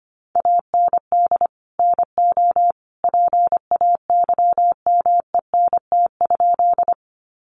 Morse code, heard as a keyed tone, about 25 wpm, 700 Hz. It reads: AND NO PAYMENT?
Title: Sentences from Top 1200 Words 25wpm